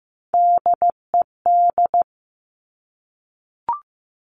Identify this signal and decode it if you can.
Morse code, heard as a keyed tone, about 15 wpm, 700 Hz. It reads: DED
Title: Top Three Letter Combinations in Words 15wpm